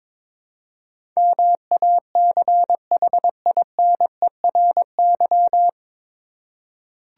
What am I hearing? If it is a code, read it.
Morse code, heard as a keyed tone, about 22 wpm, 700 Hz. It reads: MACHINERY
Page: Morse Code Ninja practice 22wpm